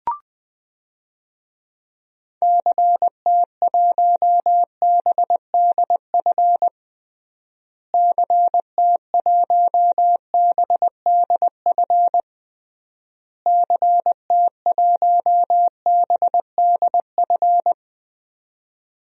Morse code, heard as a keyed tone, about 21 wpm, 700 Hz. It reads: CT1BDF CT1BDF CT1BDF
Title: DXCC-100 Call Signs - 6 Characters - Repeated 3x - 20wpm